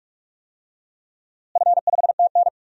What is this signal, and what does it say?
Morse code, heard as a keyed tone, about 46 wpm, 690 Hz. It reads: F5TN
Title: DXCC-100 Call Signs - 4 Characters - Repeated 3x - 45wpm